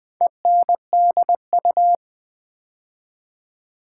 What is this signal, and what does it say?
Morse code, heard as a keyed tone, about 20 wpm, 690 Hz. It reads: ENDU
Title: Build A Word - 2 Letters At A Time - 6-Letter Words - 20wpm